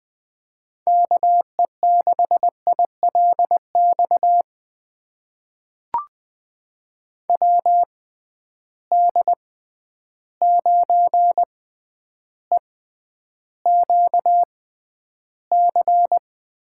Morse code, heard as a keyed tone, about 20 wpm, spaced as for 10 wpm, 700 Hz. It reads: KE6ILX WD9EQC
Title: USA Call Signs - 6 Characters - 6x Character Spacing - 20wpm